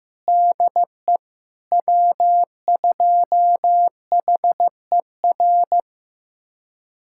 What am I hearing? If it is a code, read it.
Morse code, heard as a keyed tone, about 15 wpm, 700 Hz. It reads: DE W2HER
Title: USA Callsign DE Callsign - 5 Characters - 15wpm